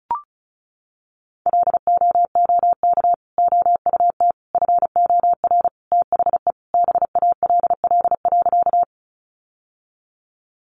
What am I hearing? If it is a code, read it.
Morse code, heard as a keyed tone, about 34 wpm, 700 Hz. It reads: LOOK OUT FOR THE BALL.
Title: Sentences from Top 900 Words 35wpm